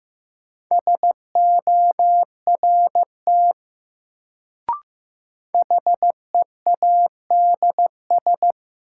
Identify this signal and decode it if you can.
Morse code, heard as a keyed tone, about 15 wpm, 700 Hz. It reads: SORT HEADS
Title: Sentences from Top 1000 Words - Review Part 3 - New Word Drill - 15wpm